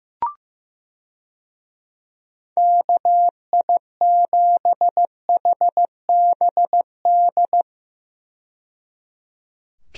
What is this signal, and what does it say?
Morse code, heard as a keyed tone, about 15 wpm, 700 Hz. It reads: KI7HBD
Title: USA Call Signs - 6 Characters - 15wpm